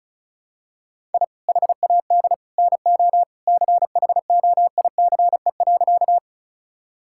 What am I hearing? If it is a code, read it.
Morse code, heard as a keyed tone, about 36 wpm, 700 Hz. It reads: I HAD NO CHOICE.
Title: Sentences from Top 1000 Words - Review Part 6 - 35wpm